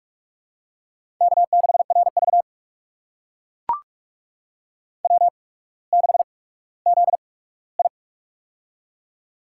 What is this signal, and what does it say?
Morse code, heard as a keyed tone, about 44 wpm, spaced as for 19 wpm, 690 Hz. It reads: K6RV W6ZI
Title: USA Call Signs - 4 Characters - 8x Character Spacing - 45wpm